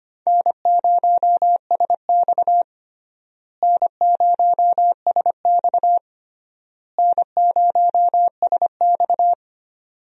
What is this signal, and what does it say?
Morse code, heard as a keyed tone, about 25 wpm, 700 Hz. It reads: N0SX N0SX N0SX
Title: USA Call Signs - 4 Characters - Repeated 3x - 25wpm